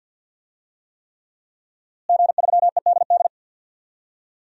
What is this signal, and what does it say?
Morse code, heard as a keyed tone, about 49 wpm, 700 Hz. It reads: G3EDD
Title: DXCC-100 Call Signs - 5 Characters - Repeated 3x - 50wpm